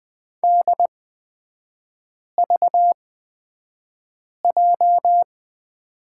Morse code, heard as a keyed tone, about 20 wpm, spaced as for 5 wpm, 690 Hz. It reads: DVJ
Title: 229 Lesson 37e - Letter x - Three Characters 20wpm